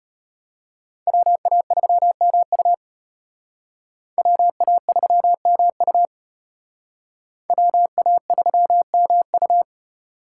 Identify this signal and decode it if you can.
Morse code, heard as a keyed tone, about 32 wpm, 700 Hz. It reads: WA3MU WA3MU WA3MU